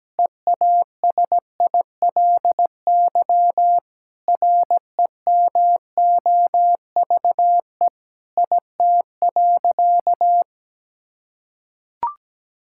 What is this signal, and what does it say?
Morse code, heard as a keyed tone, about 17 wpm, 700 Hz. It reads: EASILY REMOVE IT.
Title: Sentences from Top 1000 Words - Review Part 5 - 17wpm